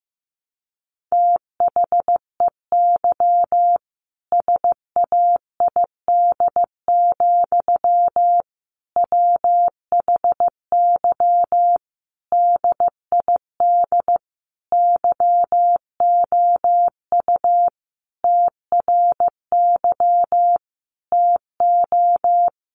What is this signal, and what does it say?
Morse code, heard as a keyed tone, about 14 wpm, 700 Hz. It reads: THEY SAID, WHY DID YOU TRY TO
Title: Sentences from Top 1200 Words 15wpm